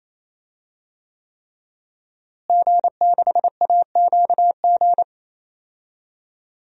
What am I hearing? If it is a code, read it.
Morse code, heard as a keyed tone, about 28 wpm, 700 Hz. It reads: G6AQG
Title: DXCC-100 Call Signs - 5 Characters - Repeated 3x - 28wpm